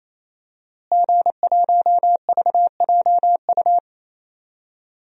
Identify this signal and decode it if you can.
Morse code, heard as a keyed tone, about 27 wpm, 700 Hz. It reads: G1VJU
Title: DXCC-100 Call Signs - 5 Characters - Repeated 3x - 28wpm